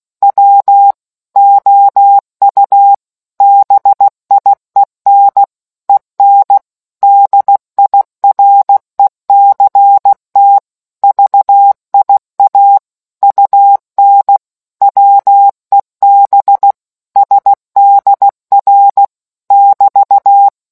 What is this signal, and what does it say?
Morse code, heard as a keyed tone, about 16 wpm, 800 Hz. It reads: W OU BIEN EN DIRECT VIA UN WEB SDR <BT>